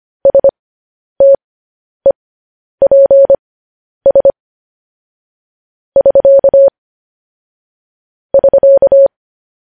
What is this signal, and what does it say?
Morse code, heard as a keyed tone, about 25 wpm, spaced as for 10 wpm, 550 Hz. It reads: STEPS <SK> <SK>
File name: podcast / News headlines-1x-25x10